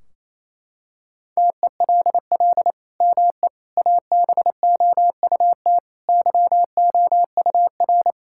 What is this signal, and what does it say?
Morse code, heard as a keyed tone, about 28 wpm, 700 Hz. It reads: TELL ME ABOUT YOUR
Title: Sentences from Top 1600 Words 28wpm